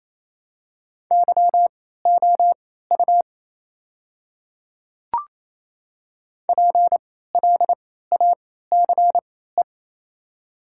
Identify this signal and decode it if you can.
Morse code, heard as a keyed tone, about 28 wpm, spaced as for 16 wpm, 700 Hz. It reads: YOU PLACE
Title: 3x Character Spacing - Top 500 Words - 28wpm